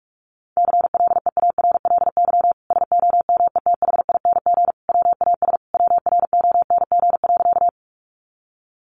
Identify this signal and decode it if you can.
Morse code, heard as a keyed tone, about 44 wpm, 700 Hz. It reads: CLEARLY SOMETHING WAS WRONG.
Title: Sentences from Top 1300 Words 45wpm